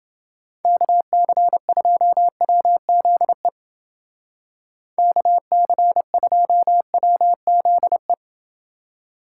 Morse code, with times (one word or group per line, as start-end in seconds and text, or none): 0.7-3.5 KC2WZE
5.0-8.1 KC2WZE